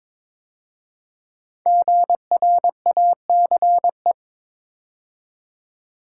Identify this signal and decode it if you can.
Morse code, heard as a keyed tone, about 23 wpm, 700 Hz. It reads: GRACE
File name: 5 Letter Words - First 3 Letters Given - 22wpm